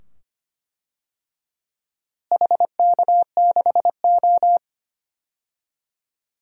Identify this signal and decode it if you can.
Morse code, heard as a keyed tone, about 25 wpm, 700 Hz. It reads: HK6O
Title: DXCC-100 Call Signs - 4 Characters - Repeated 2x - 25wpm